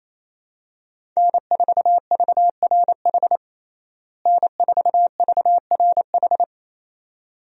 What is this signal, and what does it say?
Morse code, heard as a keyed tone, about 28 wpm, 700 Hz. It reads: N4VRH N4VRH
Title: USA Call Signs - 5 Characters - Repeated 2x - 28wpm